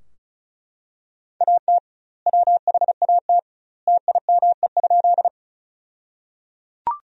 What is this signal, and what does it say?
Morse code, heard as a keyed tone, about 36 wpm, 700 Hz. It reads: AT WHAT TIME?
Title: 2x Word-Spacing - Sentences from Top 100 Words - 35wpm